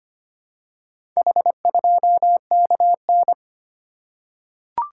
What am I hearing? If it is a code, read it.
Morse code, heard as a keyed tone, about 25 wpm, 700 Hz. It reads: H2KN
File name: practice / 25wpm / Four Letter-Number Combinations - 25wpm